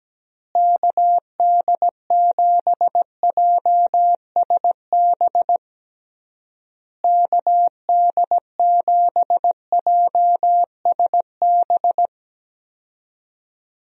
Morse code, heard as a keyed tone, about 17 wpm, 700 Hz. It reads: KD7JSB KD7JSB
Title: USA Call Signs - 6 Characters - Repeated 3x - 17wpm